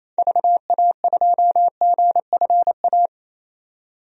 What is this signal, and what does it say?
Morse code, heard as a keyed tone, about 28 wpm, 700 Hz. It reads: VA2GFA
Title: DXCC-100 Call Signs - 6 Characters - 28wpm